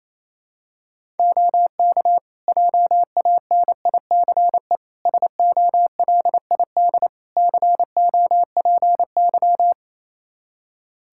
Morse code, heard as a keyed tone, about 28 wpm, 700 Hz. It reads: OK JANICE SOLID COPY
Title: K1USN SST QSO Practice 28wpm